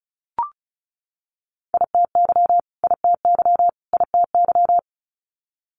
Morse code, heard as a keyed tone, about 36 wpm, 700 Hz. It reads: ITY ITY ITY
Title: Multi-letter Phonemes Repeated 3x 35wpm